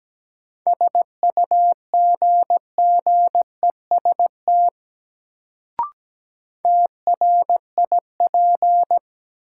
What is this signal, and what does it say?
Morse code, heard as a keyed tone, about 17 wpm, 700 Hz. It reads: SUGGEST TRIP
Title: Sentences from Top 1300 Words - New Word Drill Alt - 17wpm